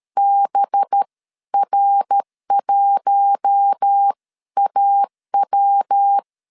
Morse code, heard as a keyed tone, about 13 wpm, spaced as for 11 wpm, 800 Hz. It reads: BR1AW
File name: LEZIONI / 8 YJ45 / 8 YJ 12wpm 2235